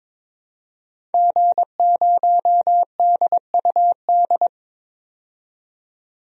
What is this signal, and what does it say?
Morse code, heard as a keyed tone, about 23 wpm, 700 Hz. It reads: G0DUD
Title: DXCC-100 Call Signs - 5 Characters - 22wpm